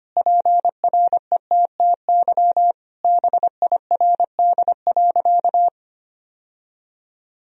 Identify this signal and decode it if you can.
Morse code, heard as a keyed tone, about 25 wpm, 700 Hz. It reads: PRETTY BIRD.
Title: Sentences from Top 1000 Words - Review Part 2 - 25wpm